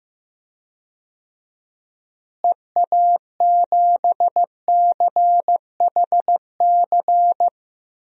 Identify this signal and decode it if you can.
Morse code, heard as a keyed tone, about 15 wpm, 700 Hz. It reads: EA7CHC
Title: DXCC-100 Call Signs - 6 Characters - Repeated 3x - 15wpm